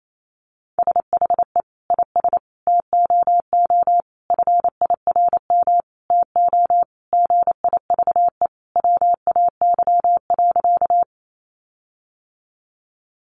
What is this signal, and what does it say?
Morse code, heard as a keyed tone, about 28 wpm, 700 Hz. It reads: SHE IS TOO FIRM TO GIVE WAY.
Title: Sentences from Top 1400 Words 28wpm